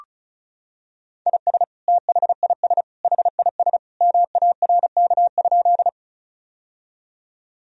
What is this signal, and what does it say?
Morse code, heard as a keyed tone, about 36 wpm, 700 Hz. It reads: IS THIS HIS MARK?
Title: Sentences from Top 1000 Words - Review Part 1 - 35wpm